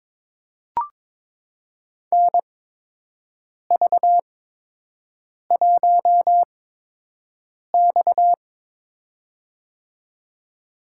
Morse code, heard as a keyed tone, about 22 wpm, spaced as for 6 wpm, 700 Hz. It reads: NV1X
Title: USA Call Signs - 4 Characters - 8x Character Spacing - 22wpm